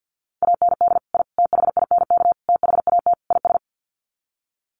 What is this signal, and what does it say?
Morse code, heard as a keyed tone, about 50 wpm, 700 Hz. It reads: AND I THINK THAT IS
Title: Sets of 5 Words Encore2 50wpm